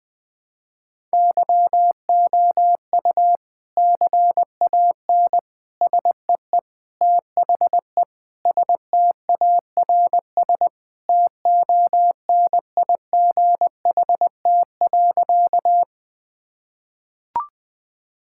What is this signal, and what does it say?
Morse code, heard as a keyed tone, about 20 wpm, 700 Hz. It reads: YOU CAN SEE THE STARS TONIGHT.
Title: Sentences with 3-Word Phrases - Spoken Beginning - 20wpm